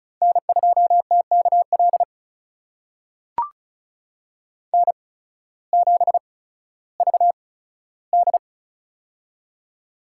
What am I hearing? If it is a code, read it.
Morse code, heard as a keyed tone, about 36 wpm, spaced as for 16 wpm, 700 Hz. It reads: N2TKL N7VD
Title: USA Call Signs - 8x Character Spacing - 35wpm